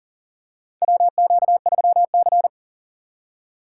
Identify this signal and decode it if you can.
Morse code, heard as a keyed tone, about 42 wpm, 690 Hz. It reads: WQ3C
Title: USA Call Signs - 2x Character Spacing - 40wpm